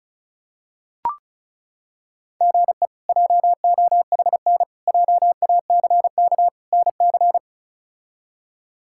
Morse code, heard as a keyed tone, about 34 wpm, 700 Hz. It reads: GE JOHN JACK NC
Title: K1USN SST QSO Practice - 35wpm